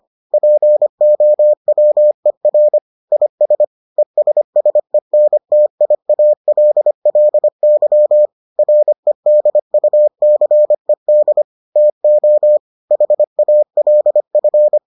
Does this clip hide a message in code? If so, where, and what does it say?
Yes: Morse code from 0.3 to 14.8 s, heard as a keyed tone, about 25 wpm, 600 Hz. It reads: POWER IS ESSENTIALLY REDUCED TO HALF